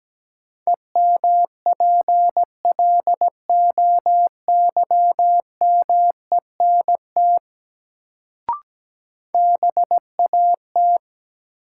Morse code, heard as a keyed tone, about 17 wpm, 700 Hz. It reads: EMPLOYMENT BAT
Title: Sentences from Top 1500 Words - New Word Drill Alt - 17wpm